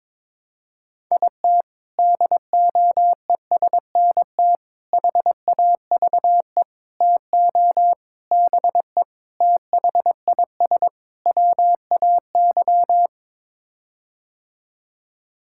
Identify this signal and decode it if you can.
Morse code, heard as a keyed tone, about 22 wpm, 700 Hz. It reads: IT DOESNT HAVE TO BE THIS WAY
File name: practice / 22wpm / Sets of 7 Words - 22wpm